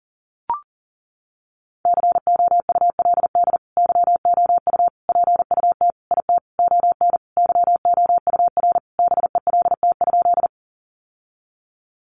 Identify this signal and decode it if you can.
Morse code, heard as a keyed tone, about 40 wpm, 700 Hz. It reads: COULD YOU PUT IT ON YOUR BELT?
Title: Sentences from Top 1400 Words 40wpm